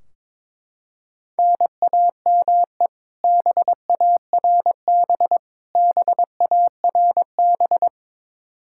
Morse code, heard as a keyed tone, about 22 wpm, 700 Hz. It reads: NAME BARB BARB
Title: Practice QSO - Exchange 1 - Beginning - 22wpm